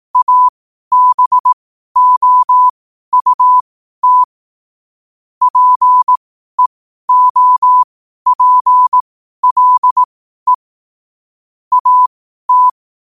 Morse code, heard as a keyed tone, about 18 wpm, spaced as for 12 wpm, 1000 Hz. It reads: ABOUT PEOPLE AT
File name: qotdfiles / 12wpm